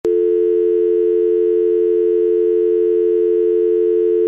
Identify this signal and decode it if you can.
Dial tone (North America), 350 + 440 Hz.
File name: iLife Sound Effects / Work - Home / Telephone Dial Tone